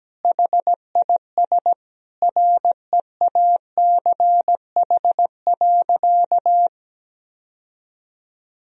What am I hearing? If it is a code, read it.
Morse code, heard as a keyed tone, about 17 wpm, 700 Hz. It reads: HIS REACH.
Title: Sentences from Top 1000 Words - Review Part 4 - 17wpm